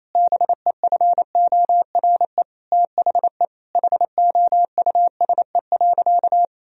Morse code, heard as a keyed tone, about 27 wpm, 700 Hz. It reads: BEFORE THE HOUSE.
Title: Sentences from Top 1000 Words - Review Part 2 - 28wpm